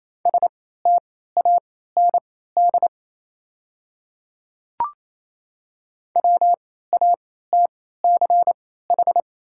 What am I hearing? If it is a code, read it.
Morse code, heard as a keyed tone, about 28 wpm, spaced as for 16 wpm, 700 Hz. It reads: STAND WATCH
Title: 3x Character Spacing - Top 300 Words - 28wpm